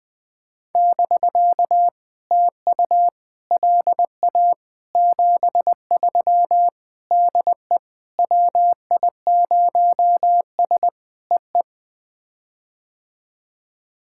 Morse code, heard as a keyed tone, about 20 wpm, 700 Hz. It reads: <BK> TU LA 73 DE WI0S EE
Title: POTA QSO Practice - 20wpm